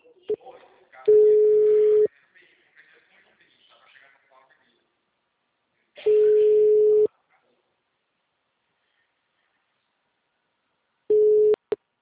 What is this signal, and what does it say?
Ringback tone (Europe, 425 Hz), with speech.